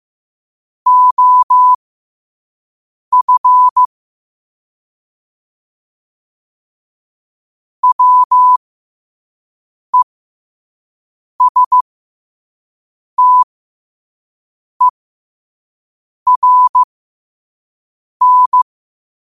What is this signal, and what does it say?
Morse code, heard as a keyed tone, about 15 wpm, spaced as for 5 wpm, 1000 Hz. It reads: OF WESTERN
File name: qotdfiles / 5wpm